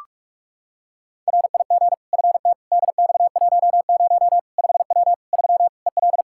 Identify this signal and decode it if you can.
Morse code, heard as a keyed tone, about 46 wpm, 700 Hz. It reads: RIG FT DX10 5W 3 EL
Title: Practice QSO - Exchange 2 - Info - 45wpm